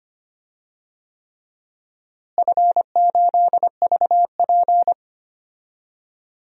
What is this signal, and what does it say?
Morse code, heard as a keyed tone, about 25 wpm, 700 Hz. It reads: F8VP
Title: DXCC-100 Call Signs - 4 Characters - Repeated 3x - 25wpm